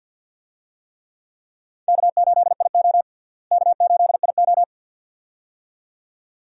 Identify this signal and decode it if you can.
Morse code, heard as a keyed tone, about 52 wpm, 700 Hz. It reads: K8IO K8IO
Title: USA Call Signs - 4 Characters - Repeated 2x - 50wpm